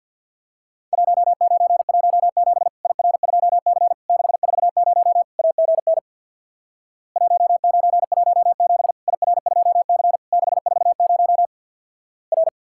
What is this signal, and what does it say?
Morse code, heard as a keyed tone, about 50 wpm, 700 Hz. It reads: 1917 IR2C 640 AGN 1917 IR2C 640 R
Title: Worked All Europe DX Contest - QTC Exchange with Repeats - 50wpm